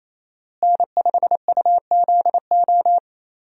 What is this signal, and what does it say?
Morse code, heard as a keyed tone, about 28 wpm, 700 Hz. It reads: N5UZO
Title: USA Call Signs - 5 Characters - Repeated 2x - 28wpm